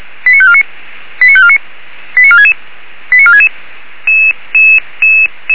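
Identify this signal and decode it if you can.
Five-tone selective call (ZVEI), heard as quick runs of five tones, tones of about 70 ms. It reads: 89549 89549 89466 89466
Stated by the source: Received with an SDR; ZVEI five-tone call 89544.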